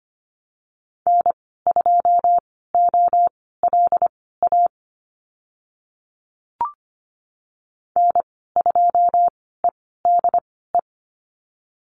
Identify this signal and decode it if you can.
Morse code, heard as a keyed tone, about 25 wpm, spaced as for 16 wpm, 700 Hz. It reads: N2OLA N2EDE
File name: Lesson 15g - Number 5 - Call Signs 25wpm